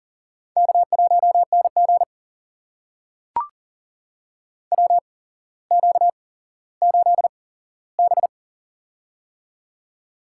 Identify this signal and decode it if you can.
Morse code, heard as a keyed tone, about 38 wpm, spaced as for 17 wpm, 700 Hz. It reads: K1NG WQ8B
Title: USA Call Signs - 4 Characters - 8x Character Spacing - 40wpm